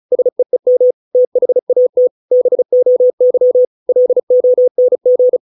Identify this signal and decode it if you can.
Morse code, heard as a keyed tone, about 34 wpm, 500 Hz. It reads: SEEM THAT BOY LONG